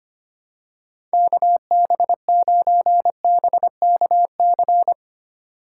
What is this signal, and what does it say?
Morse code, heard as a keyed tone, about 25 wpm, 700 Hz. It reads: KB9BKC
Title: USA Call Signs - 6 Characters - 4x Character Spacing - 25wpm